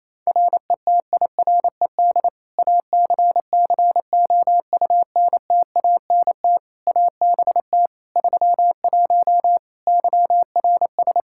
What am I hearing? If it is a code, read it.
Morse code, heard as a keyed tone, about 28 wpm, 700 Hz. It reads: RETIRED ACCOUNTANT ABT 31 YRS
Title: K1USN SST QSO Practice 28wpm